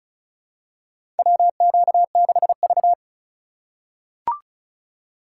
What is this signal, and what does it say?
Morse code, heard as a keyed tone, about 36 wpm, 700 Hz. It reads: WQ6V
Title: USA Call Signs - 4x Character Spacing - 35wpm